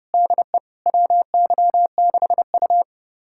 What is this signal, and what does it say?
Morse code, heard as a keyed tone, about 32 wpm, 700 Hz. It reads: DE WY6U
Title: USA Callsign DE Callsign - 4 Characters - 30wpm